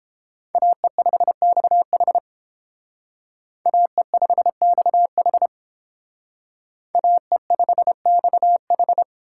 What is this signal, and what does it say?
Morse code, heard as a keyed tone, about 31 wpm, 700 Hz. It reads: AE5XH AE5XH AE5XH